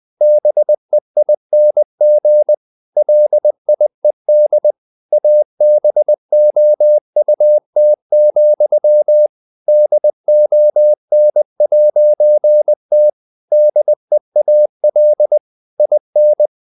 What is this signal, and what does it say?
Morse code, heard as a keyed tone, about 20 wpm, 600 Hz. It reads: BEING LIED ABOUT, DON'T DEAL IN
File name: CW / Practice / If20